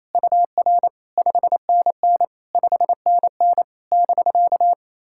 Morse code, heard as a keyed tone, about 28 wpm, 660 Hz. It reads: UR 5NN 5NN <BK>
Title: SOTA QSO Practice 28wpm